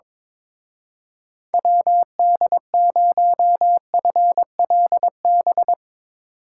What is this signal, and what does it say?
Morse code, heard as a keyed tone, about 22 wpm, 700 Hz. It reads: WD0FLB
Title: Speed Racing - USA 2x3 Call Signs 22wpm